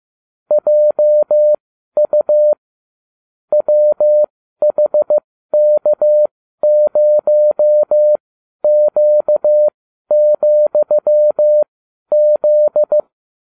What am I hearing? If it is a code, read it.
Morse code, heard as a keyed tone, about 15 wpm, spaced as for 12 wpm, 600 Hz. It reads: JU WHK0Q,Z